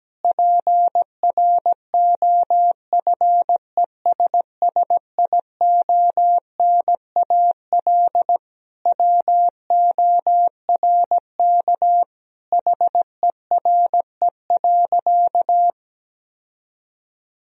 Morse code, heard as a keyed tone, about 17 wpm, 700 Hz. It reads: PROFESSIONAL WORK HERE.
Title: Sentences from Top 1000 Words - Review Part 4 - 17wpm